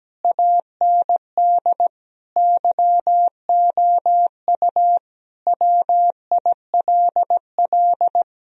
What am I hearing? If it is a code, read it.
Morse code, heard as a keyed tone, about 17 wpm, 700 Hz. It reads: AND YOU WILL
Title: Sentences from Top 1000 Words - Review Part 1 - 17wpm